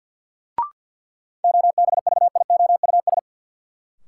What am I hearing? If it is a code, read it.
Morse code, heard as a keyed tone, about 52 wpm, 700 Hz. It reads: OBVIOUS
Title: Sentences from Top 1600 Words - New Word Drill - 50wpm